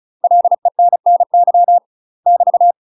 Morse code, heard as a keyed tone, about 35 wpm, 700 Hz. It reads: LENNY <BT>
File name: UK MASTER QSOs TRAINING / 35WPM / UK MASTER QSOs 0004